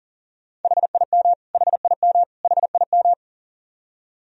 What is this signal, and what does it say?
Morse code, heard as a keyed tone, about 40 wpm, 700 Hz. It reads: HIM HIM HIM
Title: Top 100 Words Repeated 40wpm